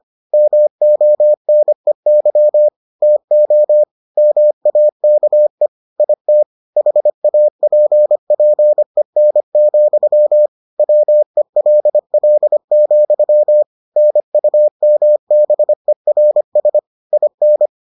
Morse code, heard as a keyed tone, about 25 wpm, 600 Hz. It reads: MONEY TO MAKE IT HAPPEN, WELL, NUMBERS IN